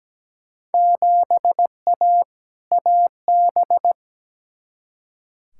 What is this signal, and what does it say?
Morse code, heard as a keyed tone, about 17 wpm, 700 Hz. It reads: 7A AB
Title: Field Day Exchanges Encore - 17wpm